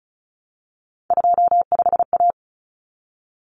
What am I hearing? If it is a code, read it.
Morse code, heard as a keyed tone, about 34 wpm, 700 Hz. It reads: 25A
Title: Common Cut Numbers - 3-Digit 35wpm